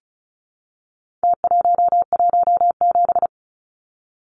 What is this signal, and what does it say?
Morse code, heard as a keyed tone, about 34 wpm, 700 Hz. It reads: T117
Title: Numbers - Contest Style - 4-Digit 35wpm